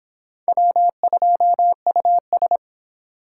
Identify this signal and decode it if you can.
Morse code, heard as a keyed tone, about 30 wpm, 700 Hz. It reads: W2US